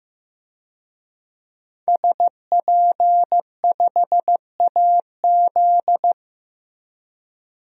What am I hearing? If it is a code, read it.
Morse code, heard as a keyed tone, about 15 wpm, 700 Hz. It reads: SP5AZ
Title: DXCC-100 Call Signs - 5 Characters - Repeated 3x - 15wpm